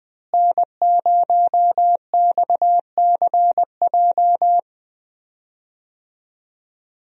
Morse code, heard as a keyed tone, about 21 wpm, 700 Hz. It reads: N0XCJ
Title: USA Call Signs - 5 Characters - Repeated 3x - 20wpm